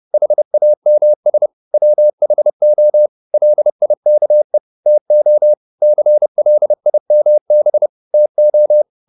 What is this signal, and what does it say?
Morse code, heard as a keyed tone, about 30 wpm, 600 Hz. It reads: HAMS WHO LIKE TO CLIMB TO